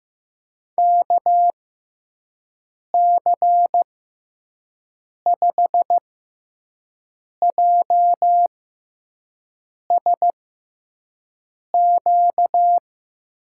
Morse code, heard as a keyed tone, about 15 wpm, spaced as for 5 wpm, 700 Hz. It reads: KC5JSQ
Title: USA Call Signs - 6x Character Spacing - 15wpm